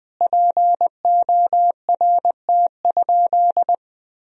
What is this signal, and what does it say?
Morse code, heard as a keyed tone, about 20 wpm, 700 Hz. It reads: PORT?
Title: Sentences from Top 1000 Words - Review Part 1 - 20wpm